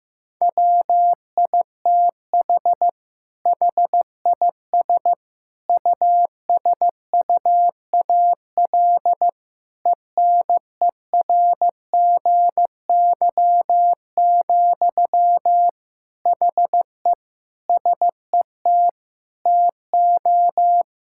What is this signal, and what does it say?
Morse code, heard as a keyed tone, about 15 wpm, 700 Hz. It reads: WITH HIS USUAL ENERGY, HE SET TO
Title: Sentences from Top 1000 Words - Review Part 6 - 15wpm